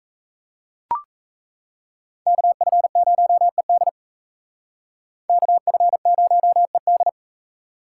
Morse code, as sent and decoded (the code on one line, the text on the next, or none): -.- ..-. ----- . -.. / -.- ..-. ----- . -..
KF0ED KF0ED